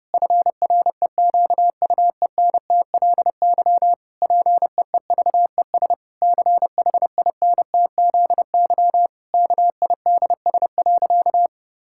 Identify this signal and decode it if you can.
Morse code, heard as a keyed tone, about 30 wpm, 700 Hz. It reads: FREQUENTLY PEEVES CHINTZY KIDS.
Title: Pangram Challenge v2 - 30wpm